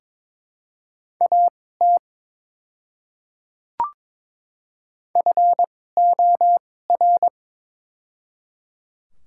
Morse code, heard as a keyed tone, about 22 wpm, spaced as for 16 wpm, 700 Hz. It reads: AT FOR
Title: 2x Character Spacing - Top 500 Words - 22wpm